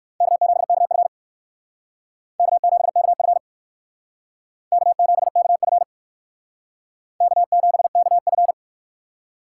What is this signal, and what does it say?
Morse code, heard as a keyed tone, about 58 wpm, 700 Hz. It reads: K7KF K7KF K7KF K7KF